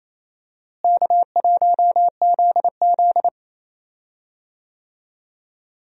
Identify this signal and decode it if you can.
Morse code, heard as a keyed tone, about 28 wpm, 700 Hz. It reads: K1ZZ
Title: USA Call Signs - 4 Characters - Repeated 2x - 28wpm